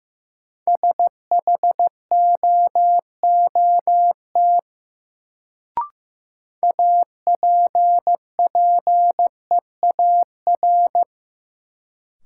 Morse code, heard as a keyed tone, about 15 wpm, 700 Hz. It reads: SHOOT APPEAR
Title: Sentences from Top 1200 Words - New Word Drill Alt - 15wpm